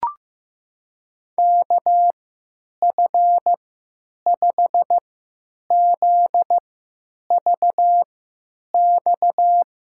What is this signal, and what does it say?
Morse code, heard as a keyed tone, about 15 wpm, spaced as for 9 wpm, 700 Hz. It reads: KF5ZVX
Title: USA Call Signs - 3x Character Spacing - 15wpm